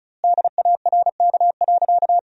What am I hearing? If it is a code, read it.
Morse code, heard as a keyed tone, about 34 wpm, 700 Hz. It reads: DARK.
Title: 2x Word-Spacing - Sentences from Top 500 Words - 35wpm